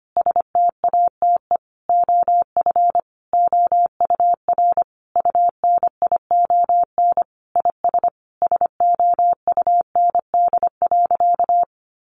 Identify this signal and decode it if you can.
Morse code, heard as a keyed tone, about 25 wpm, 700 Hz. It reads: STATE OF OUR UNION IS SOUND.
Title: Sentences from Top 1400 Words 25wpm